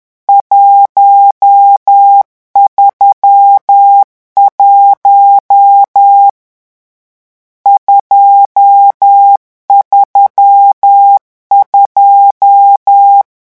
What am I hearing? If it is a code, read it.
Morse code, heard as a keyed tone, about 10 wpm, 800 Hz. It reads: 131 232
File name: morse mp3 / lecon10-vitesse 10